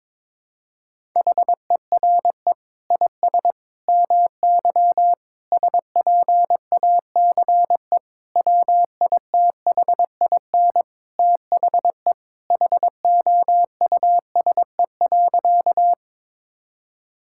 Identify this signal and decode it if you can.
Morse code, heard as a keyed tone, about 22 wpm, 700 Hz. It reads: HERE IS MY SPACE WITHIN THE HOUSE.
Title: Sentences from Top 1000 Words - Review Part 5 - 22wpm